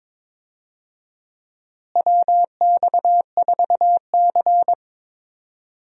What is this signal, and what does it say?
Morse code, heard as a keyed tone, about 21 wpm, 700 Hz. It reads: WX4C
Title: USA Call Signs - 4 Characters - Repeated 2x - 22wpm